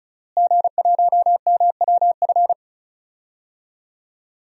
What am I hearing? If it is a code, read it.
Morse code, heard as a keyed tone, about 36 wpm, 700 Hz. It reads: G1MWF
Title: DXCC-100 Call Signs - 5 Characters - Repeated 3x - 35wpm